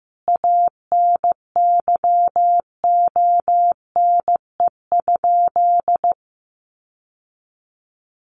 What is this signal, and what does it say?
Morse code, heard as a keyed tone, about 15 wpm, 700 Hz. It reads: ANYONE?
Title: Sentences from Top 1400 Words 15wpm